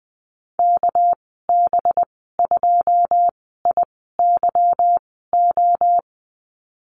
Morse code, heard as a keyed tone, about 20 wpm, spaced as for 14 wpm, 700 Hz. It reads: KB2IYO
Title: 231 Lesson 37g - Letter x - Call Signs 20wpm